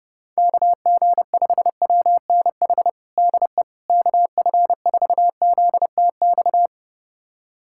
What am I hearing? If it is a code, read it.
Morse code, heard as a keyed tone, about 32 wpm, 700 Hz. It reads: KG5WNH DE KF4ZTX
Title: USA Callsign DE Callsign 30wpm